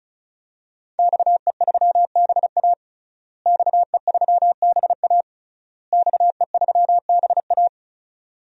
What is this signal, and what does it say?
Morse code, heard as a keyed tone, about 34 wpm, 700 Hz. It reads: XE3BA XE3BA XE3BA
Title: DXCC-100 Call Signs - 5 Characters - Repeated 3x - 35wpm